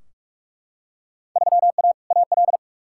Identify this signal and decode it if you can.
Morse code, heard as a keyed tone, about 44 wpm, 700 Hz. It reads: 3A AL
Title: Field Day Exchanges - 45wpm